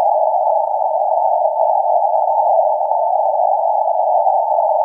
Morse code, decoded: none